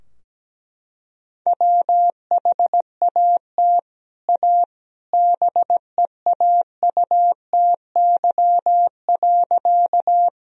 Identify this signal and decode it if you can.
Morse code, heard as a keyed tone, about 17 wpm, 700 Hz. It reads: WHAT A BEAUTY.
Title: Sentences from Top 1000 Words - Review Part 2 - 17wpm